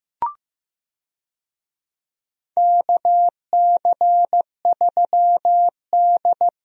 Morse code, heard as a keyed tone, about 15 wpm, 700 Hz. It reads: KC3D
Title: USA Call Signs - 4 Characters - Repeated 3x - 15wpm